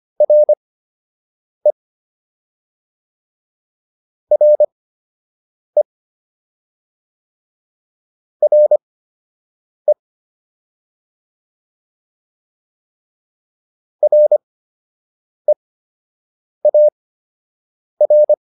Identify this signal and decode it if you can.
Morse code, heard as a keyed tone, about 25 wpm, spaced as for 7 wpm, 600 Hz. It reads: RE RE RE REAR